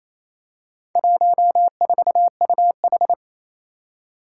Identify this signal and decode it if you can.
Morse code, heard as a keyed tone, about 28 wpm, 700 Hz. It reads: 14UH
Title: Four Challenging Letter-Number Combinations 28wpm